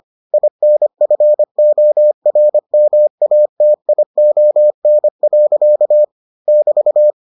Morse code, heard as a keyed tone, about 25 wpm, 600 Hz. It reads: INFORMATION. <BT>